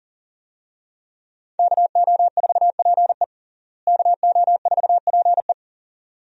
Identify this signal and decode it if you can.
Morse code, heard as a keyed tone, about 37 wpm, 700 Hz. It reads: KO4PE KO4PE
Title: USA Call Signs - 5 Characters - Repeated 2x - 40wpm